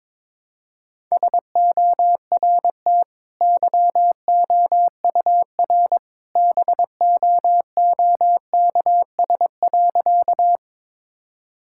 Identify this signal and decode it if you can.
Morse code, heard as a keyed tone, about 22 wpm, 700 Hz. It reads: SORT YOUR BOOKS.
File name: practice / 22wpm / Easy Sentences with Hints - 22wpm